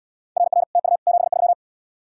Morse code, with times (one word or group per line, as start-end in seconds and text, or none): 0.4-0.6 RR
0.8-1.0 TU
1.1-1.5 73